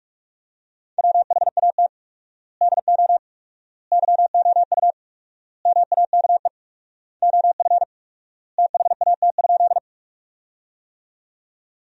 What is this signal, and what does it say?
Morse code, heard as a keyed tone, about 42 wpm, 700 Hz. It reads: WHAT DO YOU MAKE OF THAT?
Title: 4x Word-Spacing - Sentences from Top 100 Words - 45wpm